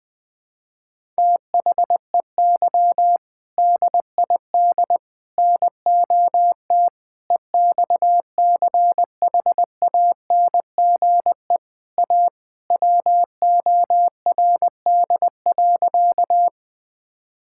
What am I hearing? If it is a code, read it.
Morse code, heard as a keyed tone, about 20 wpm, 700 Hz. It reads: THEY DID NOT EXCHANGE A WORD.
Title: Sentences from Top 1000 Words - Review Part 6 - 20wpm